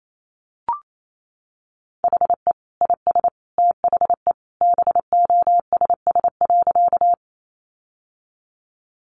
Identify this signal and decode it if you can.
Morse code, heard as a keyed tone, about 28 wpm, 700 Hz. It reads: HE IS THE BOSS.
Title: Sentences from Top 700 Words 28wpm